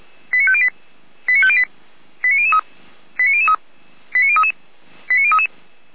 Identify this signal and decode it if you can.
Five-tone selective call (ZVEI), heard as quick runs of five tones, tones of about 80 ms. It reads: 89508 89508 89003 89003 89033 89033
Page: /455/SDR/ZVEI/89014/